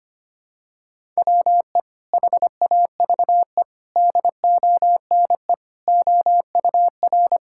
Morse code, heard as a keyed tone, about 25 wpm, 700 Hz. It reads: WE HAVE DONE OUR
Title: Sentences from Top 1000 Words - Review Part 6 - 25wpm